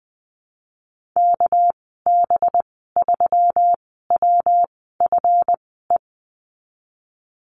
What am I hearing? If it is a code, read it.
Morse code, heard as a keyed tone, about 20 wpm, spaced as for 15 wpm, 700 Hz. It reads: KB3WFE
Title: 200 Lesson 32g - Letter k - Call Signs 20wpm